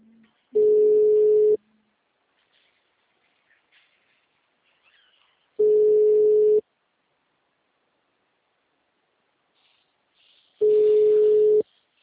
Ringback tone (Europe, 425 Hz).